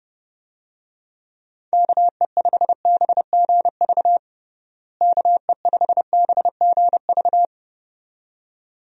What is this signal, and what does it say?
Morse code, heard as a keyed tone, about 31 wpm, 700 Hz. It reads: KE5BGV KE5BGV
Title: USA Call Signs - 6 Characters - Repeated 2x - 30wpm